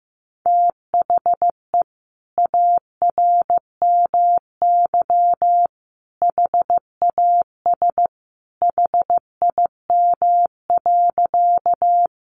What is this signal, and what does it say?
Morse code, heard as a keyed tone, about 14 wpm, 700 Hz. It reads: THE ARMY HAS HIM.
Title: Sentences from Top 800 Words 15wpm